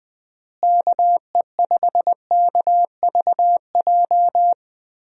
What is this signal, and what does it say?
Morse code, heard as a keyed tone, about 20 wpm, 700 Hz. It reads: KE5KVJ
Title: USA Call Signs - 6 Characters - Repeated 3x - 20wpm